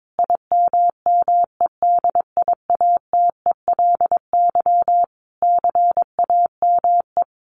Morse code, heard as a keyed tone, about 22 wpm, 700 Hz. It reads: IMMEDIATELY CAME
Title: Sentences from Top 1100 Words 22wpm